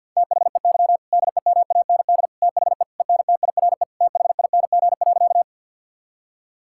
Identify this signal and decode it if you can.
Morse code, heard as a keyed tone, about 50 wpm, 700 Hz. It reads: THEY DEMAND THE ENTIRE THING.
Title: Sentences from Top 1000 Words - Review Part 5 - 50wpm